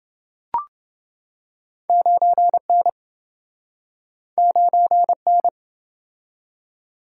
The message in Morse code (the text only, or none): 9N 9N